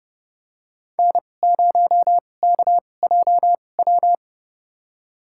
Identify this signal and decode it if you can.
Morse code, heard as a keyed tone, about 31 wpm, spaced as for 22 wpm, 700 Hz. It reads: N0KJW
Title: USA Call Signs - 5 Characters - 2x Character Spacing - 30wpm